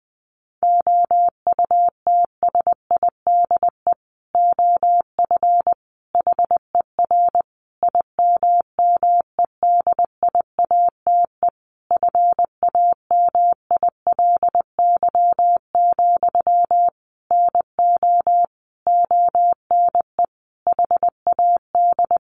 Morse code, heard as a keyed tone, about 20 wpm, 700 Hz. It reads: OUTSIDE OF HER IMMEDIATE FAMILY, NO ONE HAD
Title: Sentences from Top 1200 Words 20wpm